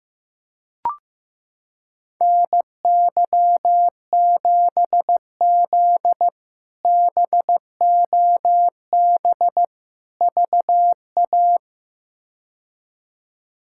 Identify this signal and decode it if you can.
Morse code, heard as a keyed tone, about 15 wpm, 700 Hz. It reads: NY7Z BOB VA
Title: K1USN SST QSO Practice - 15wpm